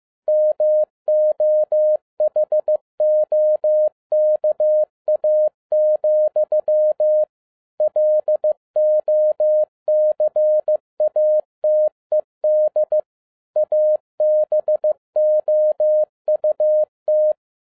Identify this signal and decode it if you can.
Morse code, heard as a keyed tone, about 15 wpm, 610 Hz. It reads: MOHOKA, LOCATED ABOUT
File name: 2011-07-mp3 / n47 JOHN K3TN 15wpm